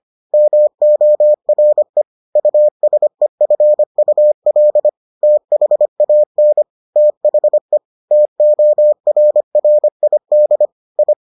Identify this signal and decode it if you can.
Morse code, heard as a keyed tone, about 25 wpm, 600 Hz. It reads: MORE USEFUL THAN THE TORRID I